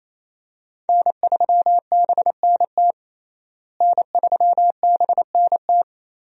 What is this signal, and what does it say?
Morse code, heard as a keyed tone, about 28 wpm, 700 Hz. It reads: N3BNT N3BNT
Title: USA Call Signs - 5 Characters - Repeated 2x - 28wpm